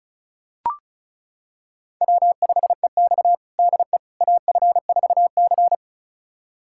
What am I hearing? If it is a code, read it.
Morse code, heard as a keyed tone, about 36 wpm, 700 Hz. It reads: W5EX DE AF4C
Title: USA Callsign DE Callsign - 4 Characters - 35wpm